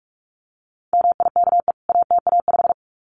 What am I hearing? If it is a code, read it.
Morse code, heard as a keyed tone, about 44 wpm, 700 Hz. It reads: MIKE ATA5
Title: CWT Exchange 45wpm